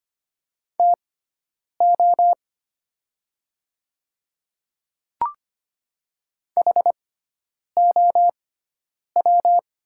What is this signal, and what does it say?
Morse code, heard as a keyed tone, about 25 wpm, spaced as for 9 wpm, 700 Hz. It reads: TO HOW
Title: 6x Character Spacing - Top 200 Words - 25wpm